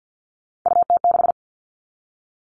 Morse code, heard as a keyed tone, about 50 wpm, 700 Hz. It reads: UT6
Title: All Cut Numbers - 3-Digit 50wpm